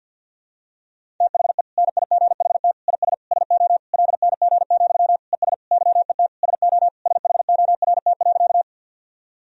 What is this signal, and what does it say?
Morse code, heard as a keyed tone, about 51 wpm, 700 Hz. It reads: THE NIGHT IS SO LNG, ES YET SO SHORT.
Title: Sentences from Top 500 Words and Abbreviations 50wpm